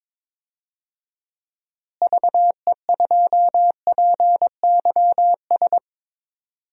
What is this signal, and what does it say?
Morse code, heard as a keyed tone, about 22 wpm, 700 Hz. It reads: VE2PYS
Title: DXCC-100 Call Signs - 6 Characters - Repeated 3x - 22wpm